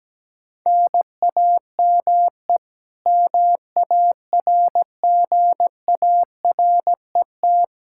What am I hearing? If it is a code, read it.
Morse code, heard as a keyed tone, about 17 wpm, 700 Hz. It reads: NAME MARGARET
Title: Practice QSO - Exchange 1 - Beginning - 17wpm